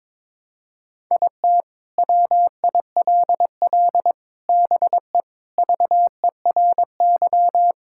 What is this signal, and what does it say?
Morse code, heard as a keyed tone, about 21 wpm, 700 Hz. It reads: IT WILL BE VERY
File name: Sentences with 7-Word Phrases - 22wpm